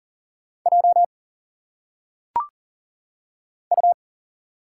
Morse code, heard as a keyed tone, about 40 wpm, 700 Hz. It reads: J U
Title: Single Letters - Phonetic Pronunciation 40wpm